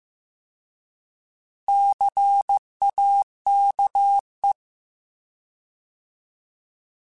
Morse code, heard as a keyed tone, about 15 wpm, 800 Hz. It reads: CAKE